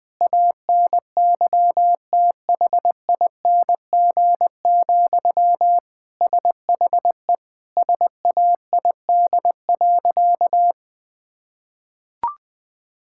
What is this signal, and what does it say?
Morse code, heard as a keyed tone, about 21 wpm, 700 Hz. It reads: ANYTHING, SHE SAID.
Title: Sentences from Top 1000 Words - Review Part 6 - 20wpm